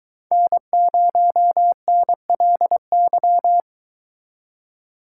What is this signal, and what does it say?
Morse code, heard as a keyed tone, about 24 wpm, 700 Hz. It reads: N0NLY